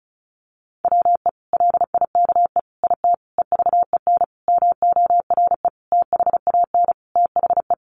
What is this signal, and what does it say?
Morse code, heard as a keyed tone, about 34 wpm, 700 Hz. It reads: WE LIKE IT EVEN MORE THAN THE
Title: Sentences from Top 1100 Words 35wpm